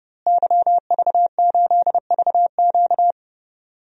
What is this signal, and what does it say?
Morse code, heard as a keyed tone, about 30 wpm, 700 Hz. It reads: YV8VQ
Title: DXCC-100 Call Signs - 5 Characters - Repeated 2x - 30wpm